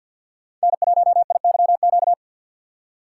Morse code, heard as a keyed tone, about 48 wpm, 700 Hz. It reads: N1IYQ
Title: USA Call Signs - 5 Characters - 6x Character Spacing - 50wpm